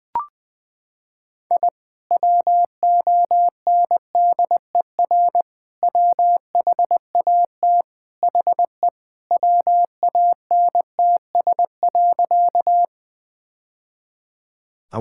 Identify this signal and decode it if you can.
Morse code, heard as a keyed tone, about 20 wpm, 700 Hz. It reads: I WONDER WHAT HE WANTS.
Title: Sentences with 2-Word Phrases - 20wpm